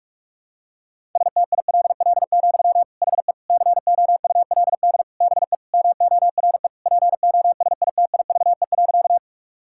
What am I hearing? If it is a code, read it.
Morse code, heard as a keyed tone, about 46 wpm, 700 Hz. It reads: STILL, HE COULD BE MORE POSITIVE.
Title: Sentences from Top 1000 Words - Review Part 5 - 45wpm